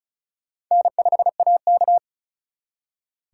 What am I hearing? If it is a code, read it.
Morse code, heard as a keyed tone, about 43 wpm, 700 Hz. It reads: N5AK
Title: Speed Racing - USA 1x2 Call Signs 35wpm